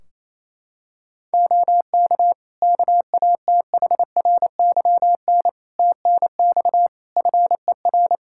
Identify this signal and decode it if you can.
Morse code, heard as a keyed tone, about 27 wpm, 700 Hz. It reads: OK KATHRYN TNX FER
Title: K1USN SST QSO Practice 28wpm